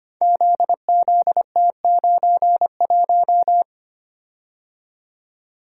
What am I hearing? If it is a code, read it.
Morse code, heard as a keyed tone, about 25 wpm, 700 Hz. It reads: ZZT91
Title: Code Groups - 5 Letter:Numbers - 25wpm